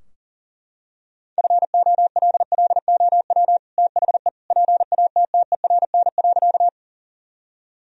Morse code, heard as a keyed tone, about 40 wpm, 700 Hz. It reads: FOLLOW THE PATTERN.
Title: Sentences from Top 1000 Words - Review Part 2 - 40wpm